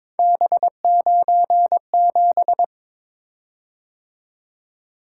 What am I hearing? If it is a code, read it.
Morse code, heard as a keyed tone, about 22 wpm, 700 Hz. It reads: B97
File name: practice / 22wpm / Three Letter-Number Combinations - 22wpm